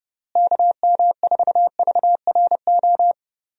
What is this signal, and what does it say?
Morse code, heard as a keyed tone, about 30 wpm, 700 Hz. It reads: KM4VRO
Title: USA Call Signs - 6 Characters - Repeated 3x - 30wpm